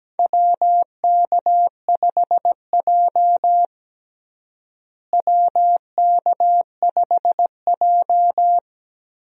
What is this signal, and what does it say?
Morse code, heard as a keyed tone, about 17 wpm, 700 Hz. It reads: WK5J WK5J
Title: USA Call Signs - 4 Characters - Repeated 3x - 17wpm